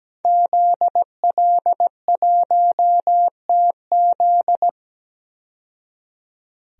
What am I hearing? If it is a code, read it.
Morse code, heard as a keyed tone, about 17 wpm, 700 Hz. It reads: ZL1TZ
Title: DXCC-100 Call Signs - 5 Characters - 17wpm